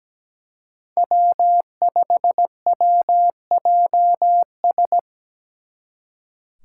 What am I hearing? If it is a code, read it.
Morse code, heard as a keyed tone, about 20 wpm, 700 Hz. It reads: W5WJS
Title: Speed Racing - USA 1x3 Call Signs 17wpm